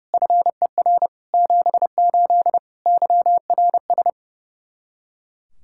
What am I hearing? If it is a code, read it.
Morse code, heard as a keyed tone, about 30 wpm, 700 Hz. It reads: FER 78 YRS
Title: K1USN SST QSO Practice 30wpm